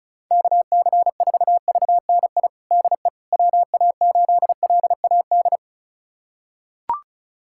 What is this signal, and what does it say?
Morse code, heard as a keyed tone, about 36 wpm, 700 Hz. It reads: KC4VNI DE WA8LAD
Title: USA Callsign DE Callsign 35wpm